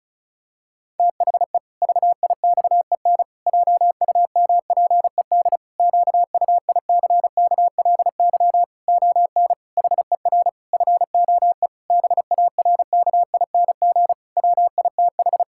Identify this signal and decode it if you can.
Morse code, heard as a keyed tone, about 36 wpm, 700 Hz. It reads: THE VIXEN JUMPED QUICKLY ON HER FOE BARKING WITH
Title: Pangram Challenge v2 - 35wpm